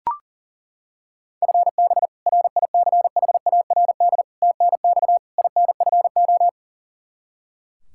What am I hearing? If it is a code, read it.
Morse code, heard as a keyed tone, about 37 wpm, 700 Hz. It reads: FB RICHARD TNX INFO
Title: Practice QSO - Exchange 3 - More Info - 40wpm